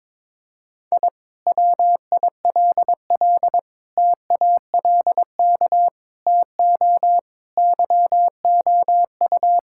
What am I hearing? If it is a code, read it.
Morse code, heard as a keyed tone, about 22 wpm, 700 Hz. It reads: I WILL TALK TO YOU
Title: Sentences from Top 1000 Words - Review Part 6 - 22wpm